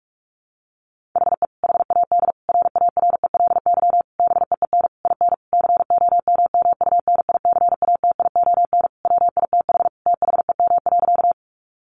Morse code, heard as a keyed tone, about 44 wpm, 700 Hz. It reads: HE HAD RARELY BEEN IN COMMUNICATION WITH THEM.
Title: Sentences from Top 1400 Words 45wpm